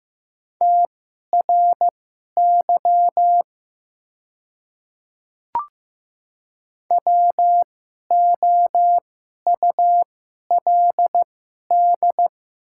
Morse code, heard as a keyed tone, about 15 wpm, spaced as for 11 wpm, 700 Hz. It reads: TRY WOULD
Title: 2x Character Spacing - Top 300 Words - 15wpm